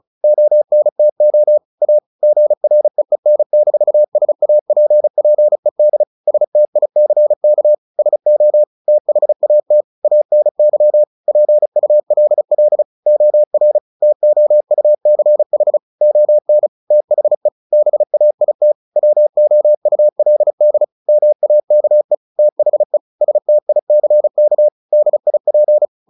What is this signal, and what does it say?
Morse code, heard as a keyed tone, about 36 wpm, 600 Hz. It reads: ONTO A GREEN-SAPPED STICK SO THAT ANY PULL OR TOUCH ON THE BAIT WOULD MAKE THE STICK DIP
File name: the old man and the sea / the old man and the sea 1 35wpm